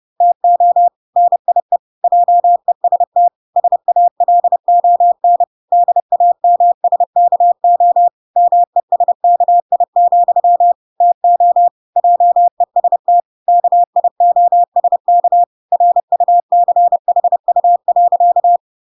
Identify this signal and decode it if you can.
Morse code, heard as a keyed tone, about 30 wpm, 700 Hz. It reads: TO NIE JEST SALON DAMSKO MESKI, TO JEST KIOSK RUCHU.